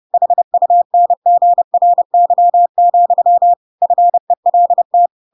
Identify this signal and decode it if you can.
Morse code, heard as a keyed tone, about 30 wpm, 700 Hz. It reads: HUNGRY, FELT
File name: Pinocchio / Pinocchio ch7 30 30 ik1hge